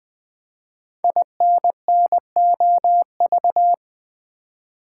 Morse code, heard as a keyed tone, about 20 wpm, 700 Hz. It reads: INNOV
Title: Build A Word - 5 Letters At A Time - 10-Letter Words - 20wpm